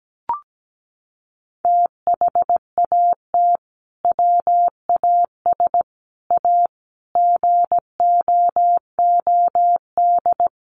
Morse code, heard as a keyed tone, about 17 wpm, 700 Hz. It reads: THAT WAS A GOOD
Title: Sentences from Top 1200 Words 17wpm